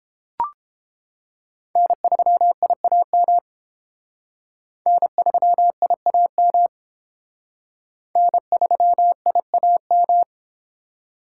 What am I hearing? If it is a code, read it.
Morse code, heard as a keyed tone, about 29 wpm, 700 Hz. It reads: N3IAM N3IAM N3IAM